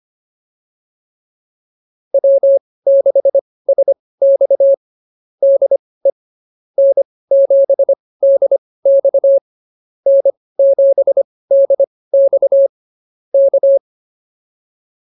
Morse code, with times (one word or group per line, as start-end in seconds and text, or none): 2.1-4.7 W6SX
5.4-6.1 DE
6.8-9.4 N7DX
10.1-12.7 N7DX
13.3-13.8 K